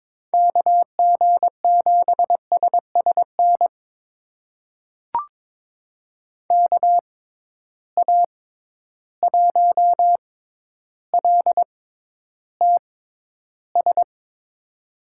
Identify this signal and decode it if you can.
Morse code, heard as a keyed tone, about 23 wpm, spaced as for 11 wpm, 700 Hz. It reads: KG7SSN KA1LTS
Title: USA Call Signs - 6x Character Spacing - 22wpm